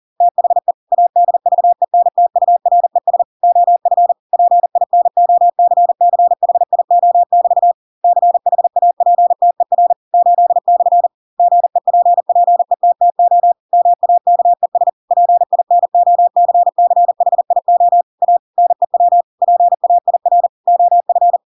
Morse code, heard as a keyed tone, about 40 wpm, 700 Hz. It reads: THE ADVENTURES OF PINOCCHIO<BT> CHAPTER 8/ GEPPETTO MAKES PINOCCHIO A NEW PAIR OF